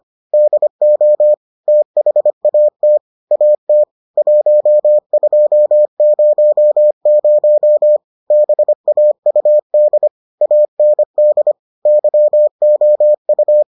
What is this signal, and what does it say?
Morse code, heard as a keyed tone, about 25 wpm, 600 Hz. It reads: DO THAT AT 1200 BAUD AND YOU